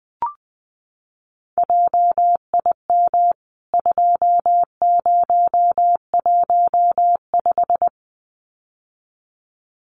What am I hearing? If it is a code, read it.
Morse code, heard as a keyed tone, about 20 wpm, 700 Hz. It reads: JIM 2015
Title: CWT Exchange 20wpm